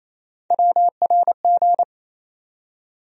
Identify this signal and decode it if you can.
Morse code, heard as a keyed tone, about 28 wpm, 700 Hz. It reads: WRG
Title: USA Call Sign Postfixes - 3 Letters 28wpm